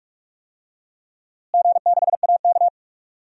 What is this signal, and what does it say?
Morse code, heard as a keyed tone, about 46 wpm, 700 Hz. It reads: G6AK
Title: DXCC-100 Call Signs - 4 Characters - Repeated 2x - 45wpm